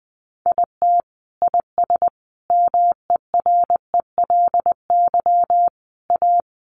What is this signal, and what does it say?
Morse code, heard as a keyed tone, about 20 wpm, 700 Hz. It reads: IT IS MERELY A
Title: Sentences from Top 1400 Words 20wpm